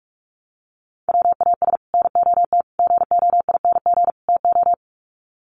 Morse code, heard as a keyed tone, about 44 wpm, 700 Hz. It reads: WAS NOT GOING TO
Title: Sets of 4 Words 45wpm